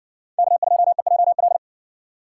Morse code, heard as a keyed tone, about 54 wpm, 700 Hz. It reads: K2EJL